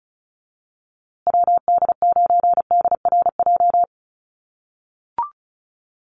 Morse code, heard as a keyed tone, about 34 wpm, 700 Hz. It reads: WD9DRJ
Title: USA Call Signs Repeated 3x 35wpm